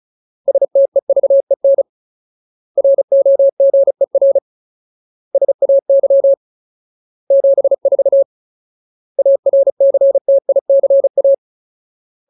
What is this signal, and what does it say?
Morse code, heard as a keyed tone, about 34 wpm, 550 Hz. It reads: STEVEN ROGER SAY 74 ARCTICA
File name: Ultimate Word List / 35WPM Ultimate